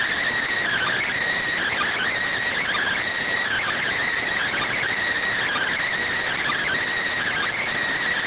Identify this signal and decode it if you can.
Emergency Alert System (SAME) tones (header bursts), heard as harsh buzzing bursts. It is heard from 0.7 to 7.4 s.